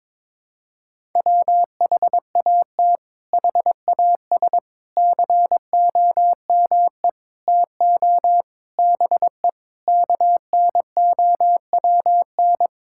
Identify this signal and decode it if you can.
Morse code, heard as a keyed tone, about 22 wpm, 700 Hz. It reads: WHAT HAS COME TO BE KNOWN
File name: practice / 22wpm / Sets of 8 Words - 22wpm